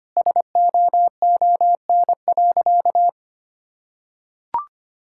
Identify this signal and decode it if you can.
Morse code, heard as a keyed tone, about 25 wpm, 700 Hz. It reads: SOON.
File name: practice / 25wpm / Sentences with 2-Word Phrases - 25wpm